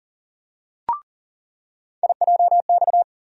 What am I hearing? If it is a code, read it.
Morse code, heard as a keyed tone, about 38 wpm, 700 Hz. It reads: IJX
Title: USA Call Sign Postfixes - 3 Letters 40wpm